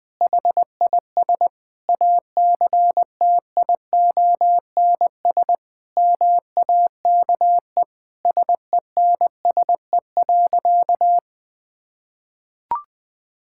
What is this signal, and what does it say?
Morse code, heard as a keyed tone, about 21 wpm, 700 Hz. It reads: HIS ACTIONS MAKE SENSE.
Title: Sentences with 5-Word Phrases - 20wpm